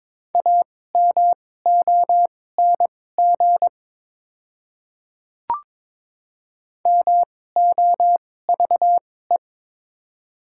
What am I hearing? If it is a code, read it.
Morse code, heard as a keyed tone, about 22 wpm, spaced as for 16 wpm, 700 Hz. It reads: AMONG MOVE
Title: 2x Character Spacing - Top 500 Words - 22wpm